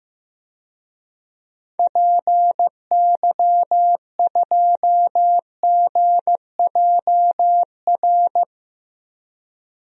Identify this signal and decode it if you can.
Morse code, heard as a keyed tone, about 15 wpm, 700 Hz. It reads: PY2GJR
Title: DXCC-100 Call Signs - 6 Characters - 15wpm